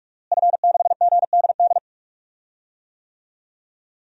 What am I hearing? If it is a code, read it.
Morse code, heard as a keyed tone, about 47 wpm, 700 Hz. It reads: F6GDD
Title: DXCC-100 Call Signs - 5 Characters - Repeated 3x - 45wpm